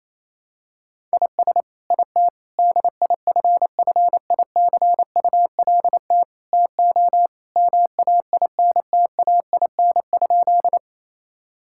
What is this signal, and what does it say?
Morse code, heard as a keyed tone, about 28 wpm, 700 Hz. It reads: IS IT DIFFICULT TO MAINTAIN?
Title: Sentences from Top 1000 Words - Review Part 5 - 28wpm